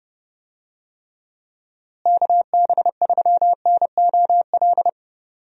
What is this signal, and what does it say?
Morse code, heard as a keyed tone, about 28 wpm, 700 Hz. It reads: KB3NOL
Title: USA Call Signs - 6 Characters - Repeated 2x - 30wpm